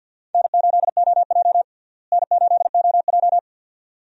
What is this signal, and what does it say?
Morse code, heard as a keyed tone, about 52 wpm, 700 Hz. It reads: N8OJ N8OJ
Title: USA Call Signs - 4 Characters - Repeated 2x - 50wpm